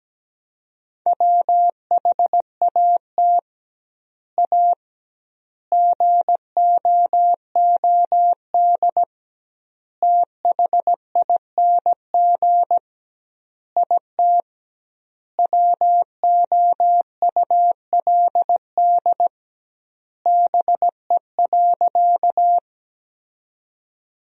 Morse code, heard as a keyed tone, about 17 wpm, 700 Hz. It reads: WHAT A GOOD THING IT WOULD BE.
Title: 2x Word-Spacing - Sentences from Top 200 Words - 17wpm